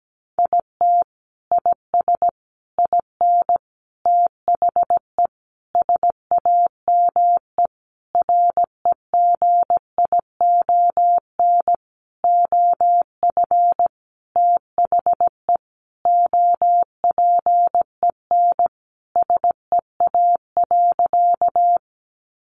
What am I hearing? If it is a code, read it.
Morse code, heard as a keyed tone, about 17 wpm, 700 Hz. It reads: IT IS IN THE SAME REGION OF THE OPEN SEA.
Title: Sentences from Top 1500 Words 17wpm